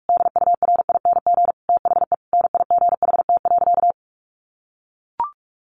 Morse code, heard as a keyed tone, about 44 wpm, 700 Hz. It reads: DURING THE NIGHT.
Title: Sentences from Top 1400 Words 45wpm